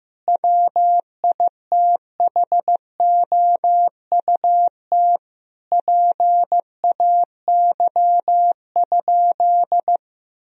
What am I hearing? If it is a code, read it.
Morse code, heard as a keyed tone, about 15 wpm, 700 Hz. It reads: WITHOUT PAY?
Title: Sentences from Top 1000 Words - Review Part 3 - 15wpm